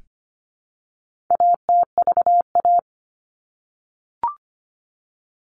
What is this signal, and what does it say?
Morse code, heard as a keyed tone, about 25 wpm, 700 Hz. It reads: ATVA
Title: All Cut Numbers - 4-Digit 25wpm